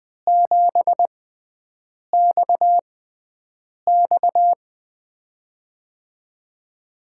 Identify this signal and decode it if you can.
Morse code, heard as a keyed tone, about 20 wpm, spaced as for 7 wpm, 700 Hz. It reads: 7XX
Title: USA Call Signs - 5 Characters - 6x Character Spacing - 20wpm